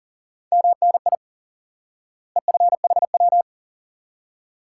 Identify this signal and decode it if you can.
Morse code, heard as a keyed tone, about 40 wpm, 700 Hz. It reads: MNI EFHW
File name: IQE Course 3 - Lesson 89 - YAGI - Two Elements - 40wpm